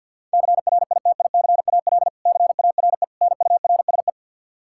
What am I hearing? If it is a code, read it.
Morse code, heard as a keyed tone, about 52 wpm, 700 Hz. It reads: CRITICAL CARE NURSE
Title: Practice QSO - Exchange 3 - More Info - 50wpm